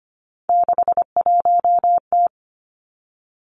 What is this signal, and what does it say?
Morse code, heard as a keyed tone, about 25 wpm, 700 Hz. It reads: 61T
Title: Three Challenging Letter-Number Combinations 25wpm